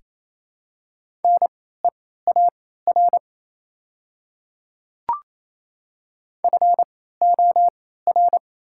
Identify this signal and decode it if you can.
Morse code, heard as a keyed tone, about 28 wpm, spaced as for 16 wpm, 700 Hz. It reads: NEAR FOR
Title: 3x Character Spacing - Top 200 Words - 28wpm